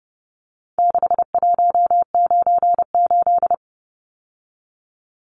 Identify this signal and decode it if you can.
Morse code, heard as a keyed tone, about 30 wpm, 700 Hz. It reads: 6198
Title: Numbers - 4-Digit 30wpm